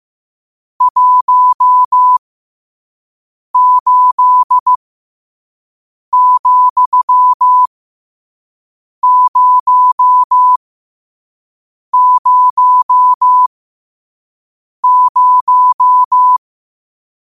Morse code, heard as a keyed tone, about 15 wpm, spaced as for 5 wpm, 1000 Hz. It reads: 18,000
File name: qotdfiles / 5wpm